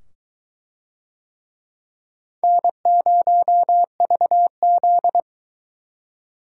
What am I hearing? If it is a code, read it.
Morse code, heard as a keyed tone, about 22 wpm, 700 Hz. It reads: N0VZ